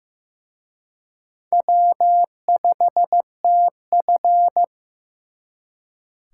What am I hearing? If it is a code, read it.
Morse code, heard as a keyed tone, about 18 wpm, 700 Hz. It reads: W5TF
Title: Speed Racing - USA Call Signs 15wpm